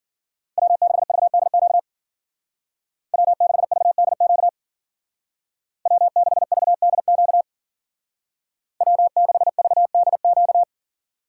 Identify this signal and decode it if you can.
Morse code, heard as a keyed tone, about 49 wpm, 700 Hz. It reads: W6VDQ W6VDQ W6VDQ W6VDQ
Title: Speed Racing - USA 1x3 Call Signs 40wpm